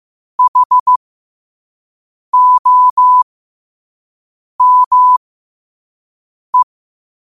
Morse code, heard as a keyed tone, about 15 wpm, spaced as for 5 wpm, 1000 Hz. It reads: HOME
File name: qotdfiles / 5wpm